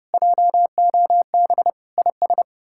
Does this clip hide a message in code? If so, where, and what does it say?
Yes: Morse code from 0.1 to 2.4 s, heard as a keyed tone, about 40 wpm, 700 Hz. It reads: JOB IS